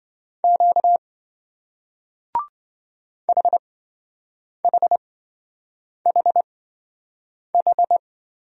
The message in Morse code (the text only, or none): Q HHHH